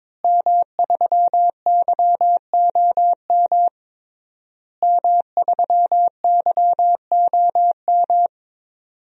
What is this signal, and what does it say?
Morse code, heard as a keyed tone, about 22 wpm, 700 Hz. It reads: M3YOM M3YOM
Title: DXCC-100 Call Signs - 5 Characters - Repeated 3x - 22wpm